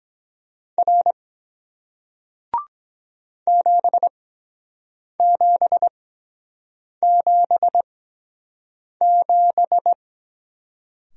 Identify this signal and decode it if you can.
Morse code, heard as a keyed tone, about 23 wpm, spaced as for 7 wpm, 700 Hz. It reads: R 7777